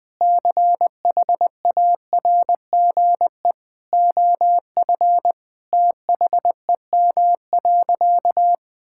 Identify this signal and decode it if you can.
Morse code, heard as a keyed tone, about 20 wpm, 700 Hz. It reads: CHARGE OF THEM.
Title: Sentences from Top 1000 Words - Review Part 6 - 20wpm